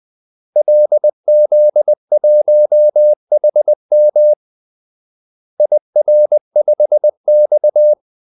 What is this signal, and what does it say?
Morse code, heard as a keyed tone, about 20 wpm, 600 Hz. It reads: LZ1HM IR5X
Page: Kaldesignaler 20 wpm